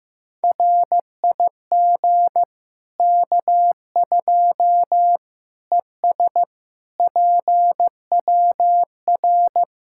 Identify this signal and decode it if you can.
Morse code, heard as a keyed tone, about 15 wpm, 700 Hz. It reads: RIG K2 ES PWR
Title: Practice QSO - Exchange 2 - Info - 15wpm